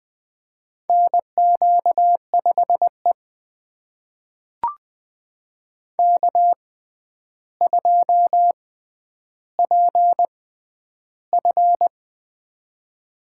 Morse code, heard as a keyed tone, about 21 wpm, spaced as for 10 wpm, 700 Hz. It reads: NQ5E K2PF
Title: USA Call Signs - 4 Characters - 6x Character Spacing - 20wpm